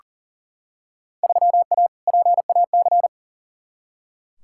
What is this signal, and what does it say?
Morse code, heard as a keyed tone, about 40 wpm, 700 Hz. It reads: 3A PAC
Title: Field Day Exchanges - 40wpm